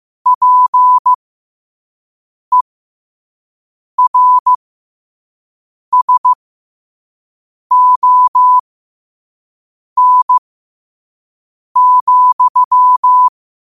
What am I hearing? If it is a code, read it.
Morse code, heard as a keyed tone, about 15 wpm, spaced as for 5 wpm, 1000 Hz. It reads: PERSON,
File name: qotdfiles / 5wpm